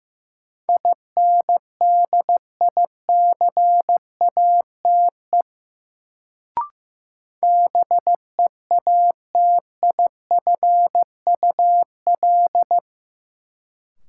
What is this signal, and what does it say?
Morse code, heard as a keyed tone, about 15 wpm, 700 Hz. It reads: INDICATE BEATIFUL
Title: Sentences from Top 1300 Words - New Word Drill Alt - 15wpm